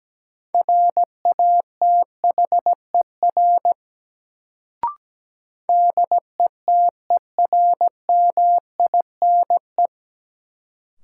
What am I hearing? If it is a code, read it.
Morse code, heard as a keyed tone, about 17 wpm, 700 Hz. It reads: RATHER DETERMINE
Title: Sentences from Top 700 Words - New Word Drill - 17wpm